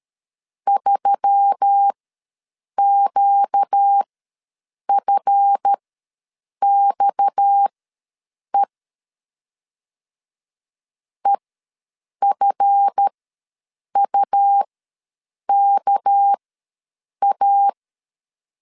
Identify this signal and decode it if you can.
Morse code, heard as a keyed tone, about 13 wpm, spaced as for 7 wpm, 800 Hz. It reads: 3QFXE EFUKA
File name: LEZIONI / 7 GF9Z / 7 GF ws 12wpm 3140